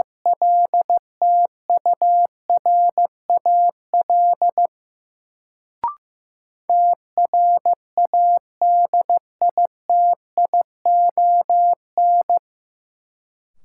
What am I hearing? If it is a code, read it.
Morse code, heard as a keyed tone, about 15 wpm, 700 Hz. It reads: LTURAL TRADITION
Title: Sentences from Top 1500 Words - New Word Drill - 15wpm